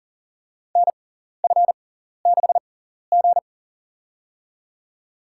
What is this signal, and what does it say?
Morse code, heard as a keyed tone, about 40 wpm, spaced as for 14 wpm, 700 Hz. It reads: NF6G
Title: USA Call Signs - 4 Characters - 6x Character Spacing - 40wpm